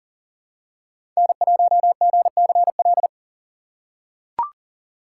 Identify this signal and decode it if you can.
Morse code, heard as a keyed tone, about 40 wpm, 700 Hz. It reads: N1GCL
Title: USA Call Signs - 5 Characters - 4x Character Spacing - 40wpm